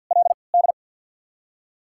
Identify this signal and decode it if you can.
Morse code, heard as a keyed tone, about 48 wpm, spaced as for 28 wpm, 700 Hz. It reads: LD
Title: 3x Character Spacing - Top 500 Words - 50wpm